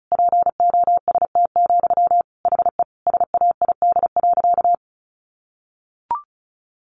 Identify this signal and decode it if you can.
Morse code, heard as a keyed tone, about 34 wpm, 700 Hz. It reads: POST, HE SAID.
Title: Sentences from Top 1000 Words 35wpm